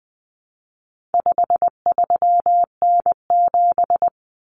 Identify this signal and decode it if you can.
Morse code, heard as a keyed tone, about 20 wpm, 700 Hz. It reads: 53N7
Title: Common Cut Numbers - 4-Digit 20wpm